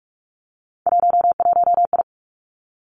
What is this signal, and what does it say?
Morse code, heard as a keyed tone, about 46 wpm, 700 Hz. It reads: 11I
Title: Three Challenging Letter-Number Combinations 45wpm